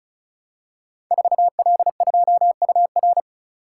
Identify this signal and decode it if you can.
Morse code, heard as a keyed tone, about 36 wpm, 700 Hz. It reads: 4L2UR
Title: Code Groups - 5 Letter:Numbers - 35wpm